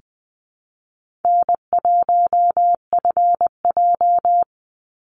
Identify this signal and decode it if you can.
Morse code, heard as a keyed tone, about 19 wpm, 700 Hz. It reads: N1FJ
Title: NA SOTA Activator Callsigns 20wpm